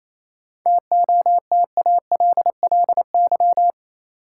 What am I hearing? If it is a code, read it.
Morse code, heard as a keyed tone, about 27 wpm, 700 Hz. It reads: TOTALLY
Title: Sentences from Top 1600 Words - New Word Drill Alt - 28wpm